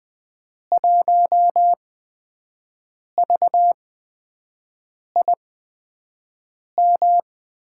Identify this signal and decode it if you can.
Morse code, heard as a keyed tone, about 20 wpm, spaced as for 5 wpm, 700 Hz. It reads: 1VIM
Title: USA Call Signs - 8x Character Spacing - 20wpm